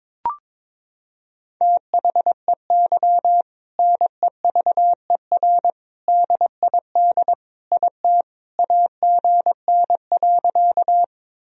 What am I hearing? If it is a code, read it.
Morse code, heard as a keyed tone, about 21 wpm, 700 Hz. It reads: THEY NEVER DID IT AGN.
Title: Sentences from Top 300 Words and Abbreviations 22wpm